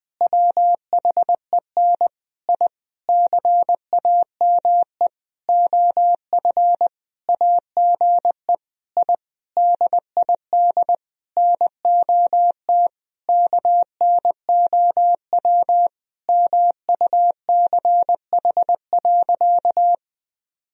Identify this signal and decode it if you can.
Morse code, heard as a keyed tone, about 20 wpm, 700 Hz. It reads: WHEN I CAME OF AGE I DID NOT KNOW MUCH.
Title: Sentences from Top 1000 Words - Review Part 3 - 20wpm